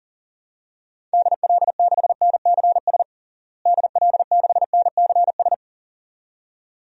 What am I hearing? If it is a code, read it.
Morse code, heard as a keyed tone, about 40 wpm, 700 Hz. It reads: DL6NCS DL6NCS
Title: DXCC-100 Call Signs - 6 Characters - Repeated 2x - 40wpm